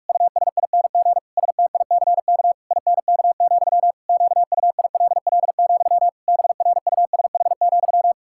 Morse code, heard as a keyed tone, about 46 wpm, 700 Hz. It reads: USING STICK INK, QUILL, BRUSH,
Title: Pangram Challenge v2 - 45wpm